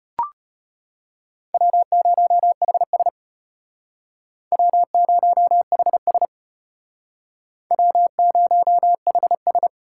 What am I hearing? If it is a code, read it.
Morse code, heard as a keyed tone, about 32 wpm, 700 Hz. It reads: W0HS W0HS W0HS